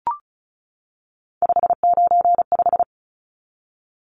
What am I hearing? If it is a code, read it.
Morse code, heard as a keyed tone, about 34 wpm, 700 Hz. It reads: 595
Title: Numbers - 3-Digit 35wpm